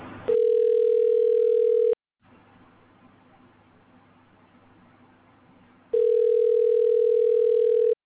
Ringback tone (North America), 440 + 480 Hz.